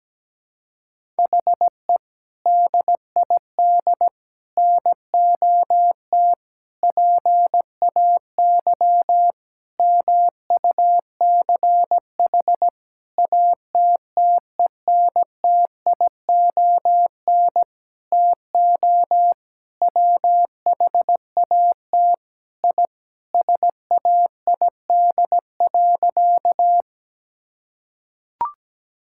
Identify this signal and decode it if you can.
Morse code, heard as a keyed tone, about 17 wpm, 700 Hz. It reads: HE DID NOT PAY MUCH ATTENTION TO WHAT I SAID.
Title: Sentences from Top 1000 Words - Review Part 6 - 17wpm